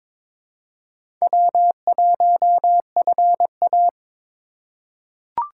Morse code, heard as a keyed tone, about 22 wpm, 700 Hz. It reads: W1FA
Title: USA Call Signs - 4 Characters - 3x Character Spacing - 22wpm